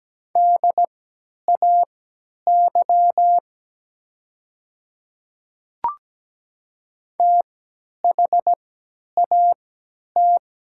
Morse code, heard as a keyed tone, about 17 wpm, spaced as for 10 wpm, 700 Hz. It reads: DAY THAT
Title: 3x Character Spacing - Top 100 Words - 17wpm